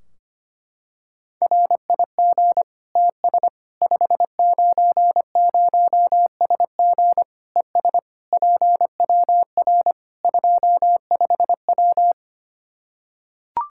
Morse code, heard as a keyed tone, about 25 wpm, 700 Hz. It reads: RIG TS 590SG ES PWR 25W
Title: K1USN SST QSO Practice 25wpm